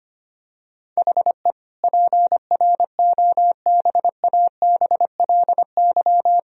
Morse code, heard as a keyed tone, about 25 wpm, 700 Hz. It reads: HE PROBABLY
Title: Sentences from Top 1000 Words - Review Part 4 - 25wpm